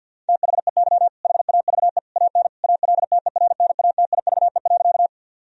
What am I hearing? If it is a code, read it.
Morse code, heard as a keyed tone, about 50 wpm, 700 Hz. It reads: THEY HAVE AN ALTERNATIVE.
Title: Sentences from Top 1000 Words - Review Part 4 - 50wpm